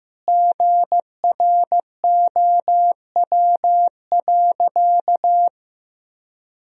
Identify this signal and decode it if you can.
Morse code, heard as a keyed tone, about 15 wpm, 700 Hz. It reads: GROW.
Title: Sentences from Top 1000 Words - Review Part 2 - 15wpm